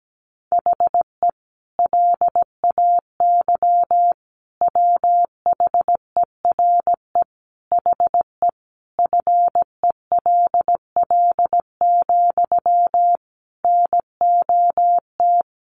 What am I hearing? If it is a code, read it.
Morse code, heard as a keyed tone, about 17 wpm, 700 Hz. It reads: HE LAY WHERE HE FELL, NOT
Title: A Sound Of Thunder 17wpm